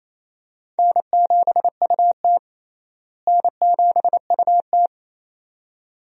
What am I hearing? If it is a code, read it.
Morse code, heard as a keyed tone, about 28 wpm, 700 Hz. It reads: N7UT N7UT
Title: USA Call Signs - 4 Characters - Repeated 2x - 28wpm